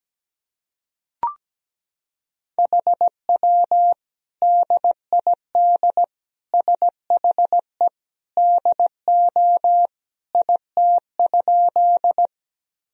Morse code, heard as a keyed tone, about 17 wpm, 700 Hz. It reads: HW DID SHE DO IT?
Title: Sentences from Top 100 Words and Abbreviations 17wpm